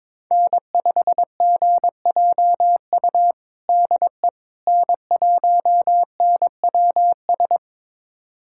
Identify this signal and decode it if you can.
Morse code, heard as a keyed tone, about 21 wpm, 700 Hz. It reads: N5GJU DE N1NWS
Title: USA Callsign DE Callsign - 5 Characters - 22wpm